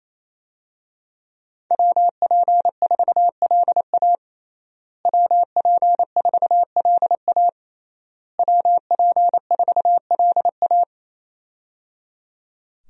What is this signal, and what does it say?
Morse code, heard as a keyed tone, about 28 wpm, 700 Hz. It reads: WP4LA WP4LA WP4LA
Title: USA Call Signs - 5 Characters - Repeated 3x - 28wpm